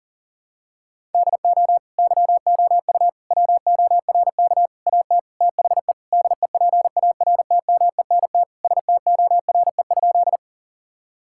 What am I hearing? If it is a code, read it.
Morse code, heard as a keyed tone, about 40 wpm, 700 Hz. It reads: DO YOU WORK AT THE DEPARTMENT STORE?
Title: Sentences from Top 1000 Words - Review Part 5 - 40wpm